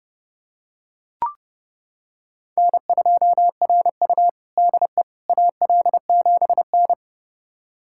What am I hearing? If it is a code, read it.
Morse code, heard as a keyed tone, about 32 wpm, 700 Hz. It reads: N2RU DE AL7N
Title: USA Callsign DE Callsign - 4 Characters - 30wpm